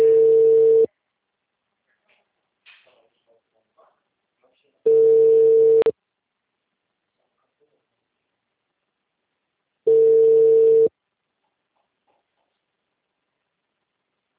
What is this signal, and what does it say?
Ringback tone (China, 450 Hz), with speech.